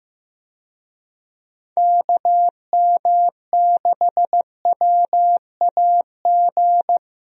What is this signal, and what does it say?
Morse code, heard as a keyed tone, about 15 wpm, 700 Hz. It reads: KM6WAG
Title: USA Call Signs - 6 Characters - Repeated 3x - 15wpm